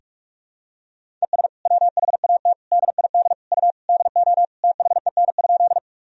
Morse code, heard as a keyed tone, about 46 wpm, 700 Hz. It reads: ES WHAT DID U DO THEN?
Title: Sentences from Top 100 Words and Abbreviations 45wpm